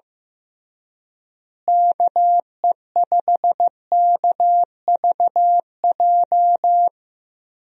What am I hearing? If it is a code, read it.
Morse code, heard as a keyed tone, about 15 wpm, 700 Hz. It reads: KE5KVJ
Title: USA Call Signs - 6 Characters - Repeated 3x - 15wpm